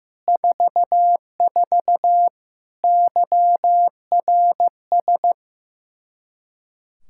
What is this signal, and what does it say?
Morse code, heard as a keyed tone, about 15 wpm, 700 Hz. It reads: 44 YRS
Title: Practice QSO - Exchange 3 - More Info - 15wpm